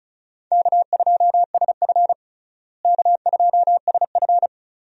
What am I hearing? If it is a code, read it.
Morse code, heard as a keyed tone, about 36 wpm, 700 Hz. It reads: K2SF K2SF
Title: USA Call Signs - 4 Characters - Repeated 3x - 35wpm